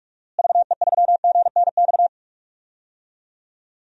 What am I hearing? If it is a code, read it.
Morse code, heard as a keyed tone, about 46 wpm, 700 Hz. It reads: VE3GNX
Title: DXCC-100 Call Signs - 6 Characters - Repeated 2x - 45wpm